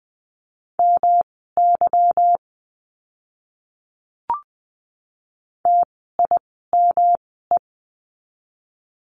Morse code, heard as a keyed tone, about 20 wpm, spaced as for 14 wpm, 700 Hz. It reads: MY TIME
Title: 160 Lesson 26f - Letter g - Words 20wpm